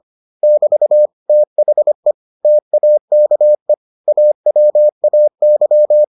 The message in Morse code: <BT> THE TAKE AWAY

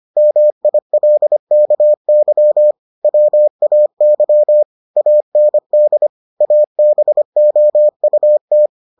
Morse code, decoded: MILKY WAY AND ABOUT

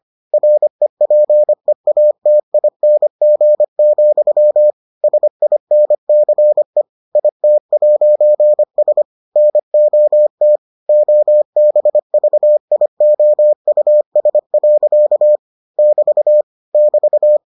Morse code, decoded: REPEATING, SINCE IT'S NOT OBVIOUS. <BT> <BT>

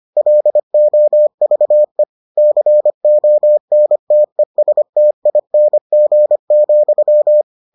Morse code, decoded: LOVE CONTESTING,

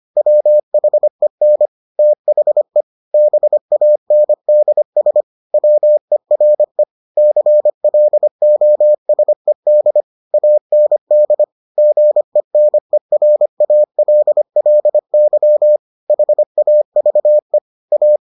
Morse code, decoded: WHEN THE BANDS WERE CLOSED AND GENERALLY HAVE A

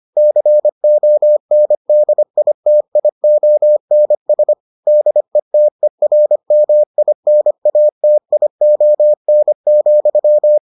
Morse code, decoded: CONDITIONS DETERMINATION,

